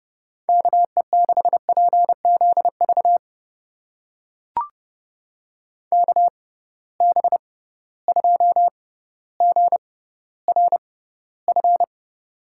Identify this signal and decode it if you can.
Morse code, heard as a keyed tone, about 30 wpm, spaced as for 15 wpm, 700 Hz. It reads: KE6PZV KB2GRF
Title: USA Call Signs - 6 Characters - 6x Character Spacing - 30wpm